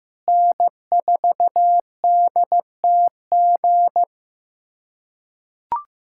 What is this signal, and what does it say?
Morse code, heard as a keyed tone, about 15 wpm, 700 Hz. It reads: N4DTG
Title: USA Callsign DE Callsign 15wpm